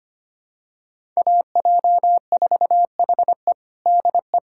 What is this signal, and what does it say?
Morse code, heard as a keyed tone, about 25 wpm, 700 Hz. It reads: AJ4HE DE
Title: USA Callsign DE Callsign - 5 Characters - 25wpm